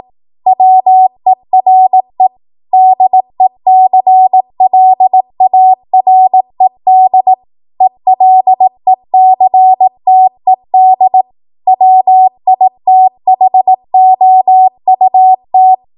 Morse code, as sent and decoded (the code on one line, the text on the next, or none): .-- . .-. / -.. -.-. .-.. .- .-. -.. / . .-.. -.-. - . -.. / .-- .. - .... --- ..- -
WERE DECLARED ELECTED WITHOUT